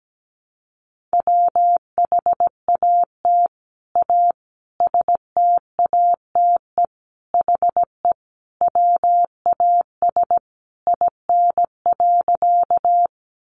Morse code, heard as a keyed tone, about 17 wpm, 700 Hz. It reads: WHAT A STATE HE WAS IN.
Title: Sentences from Top 400 Words 17wpm